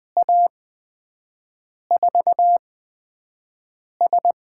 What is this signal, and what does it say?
Morse code, heard as a keyed tone, about 20 wpm, spaced as for 5 wpm, 700 Hz. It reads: A4S
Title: USA Call Signs - 5 Characters - 8x Character Spacing - 20wpm